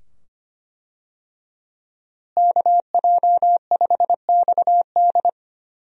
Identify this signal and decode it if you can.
Morse code, heard as a keyed tone, about 25 wpm, 700 Hz. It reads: KJ5XD
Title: USA Call Signs - 5 Characters - Repeated 2x - 25wpm